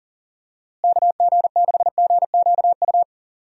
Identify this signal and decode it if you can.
Morse code, heard as a keyed tone, about 40 wpm, 700 Hz. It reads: KG6GQU
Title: USA Call Signs - 6 Characters - Repeated 3x - 40wpm